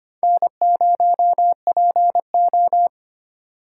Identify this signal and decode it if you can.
Morse code, heard as a keyed tone, about 31 wpm, 700 Hz. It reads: N0PO
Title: Speed Racing - USA 1x2 Call Signs 25wpm